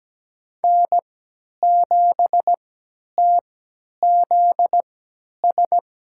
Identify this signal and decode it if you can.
Morse code, heard as a keyed tone, about 17 wpm, spaced as for 10 wpm, 700 Hz. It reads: N7TZS
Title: USA Call Signs - 5 Characters - 3x Character Spacing - 17wpm